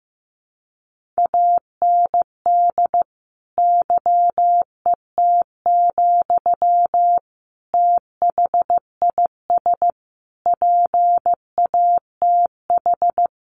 Morse code, heard as a keyed tone, about 15 wpm, 700 Hz. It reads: AND YET, THIS PATH
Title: Sentences from Top 1400 Words 15wpm